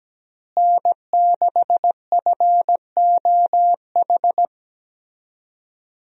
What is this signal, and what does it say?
Morse code, heard as a keyed tone, about 17 wpm, 700 Hz. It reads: N6FOH
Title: USA Call Signs - 5 Characters - Repeated 3x - 17wpm